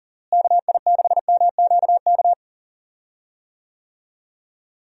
Morse code, heard as a keyed tone, about 40 wpm, 700 Hz. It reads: KI6MQK
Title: USA Call Signs - 6 Characters - Repeated 3x - 40wpm